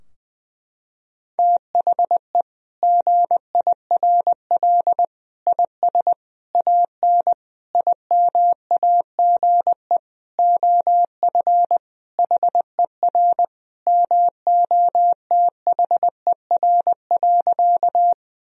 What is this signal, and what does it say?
Morse code, heard as a keyed tone, about 20 wpm, 700 Hz. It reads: THE GIRL IS AN IMAGE OF HER MOTHER.
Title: Sentences from Top 1000 Words - Review Part 4 - 20wpm